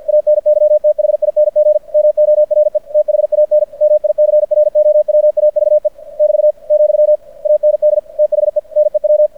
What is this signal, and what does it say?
Morse code, heard as a keyed tone, about 50 wpm, 600 Hz. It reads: ANYTHING MORE THAN MICROWAVE <BT> , AND THE NEW
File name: archived / CRSnet